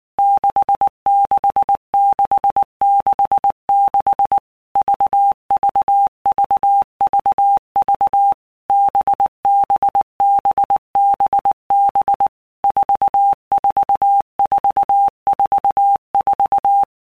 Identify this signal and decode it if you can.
Morse code, heard as a keyed tone, about 19 wpm, 800 Hz. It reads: 66666 VVVVV BBBBB 44444
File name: Basic Letter Groups By Blocks / Groups@18 / VB46@18